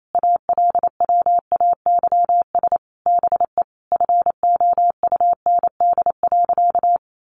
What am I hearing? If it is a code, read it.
Morse code, heard as a keyed tone, about 28 wpm, 700 Hz. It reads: ALWAYS BE FOUND.
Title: Sentences from Top 1300 Words 28wpm